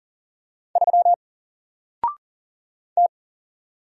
Morse code, heard as a keyed tone, about 44 wpm, 700 Hz. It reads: 3 T
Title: Single Character (Letter:Number:Punctuation:Prosign) 40wpm